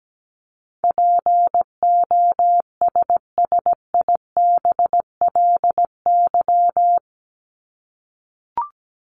Morse code, heard as a keyed tone, about 17 wpm, 700 Hz. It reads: POSSIBLY
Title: LY Suffix - Top 100 Words with Base Words 17wpm